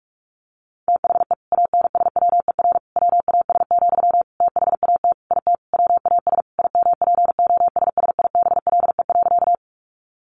Morse code, heard as a keyed tone, about 44 wpm, 700 Hz. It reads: THE ANSWER WAS, THAT IT WAS IMPOSSIBLE.
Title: Sentences from Top 1100 Words 45wpm